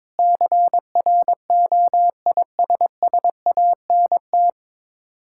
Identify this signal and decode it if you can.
Morse code, heard as a keyed tone, about 21 wpm, 690 Hz. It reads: CROISSANT
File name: Difficult Words with Hints - 22wpm